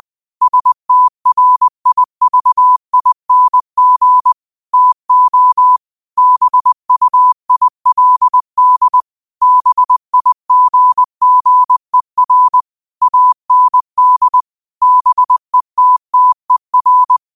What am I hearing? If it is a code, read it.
Morse code, heard as a keyed tone, about 20 wpm, 1000 Hz. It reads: STRIVING TO BUILD BIGGER AND BETTER
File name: qotdfiles / 20wpm